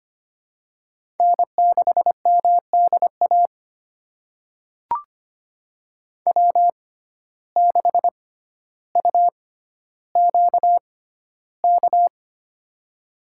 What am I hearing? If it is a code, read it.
Morse code, heard as a keyed tone, about 25 wpm, spaced as for 13 wpm, 700 Hz. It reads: N6MDA W6UQK
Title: USA Call Signs - 5 Characters - 6x Character Spacing - 25wpm